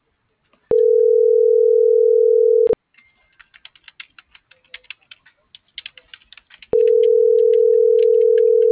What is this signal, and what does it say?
Ringback tone (North America), 440 + 480 Hz.